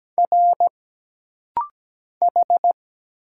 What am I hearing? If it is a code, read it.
Morse code, heard as a keyed tone, about 17 wpm, 700 Hz. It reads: R H
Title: Single Letters - Rapid-Fire 17wpm